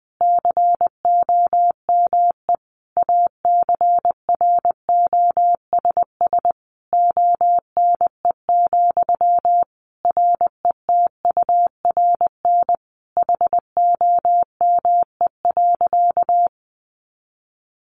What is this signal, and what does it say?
Morse code, heard as a keyed tone, about 20 wpm, 700 Hz. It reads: COME ACROSS ONE, RETURN HOME.
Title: Sentences from Top 800 Words 20wpm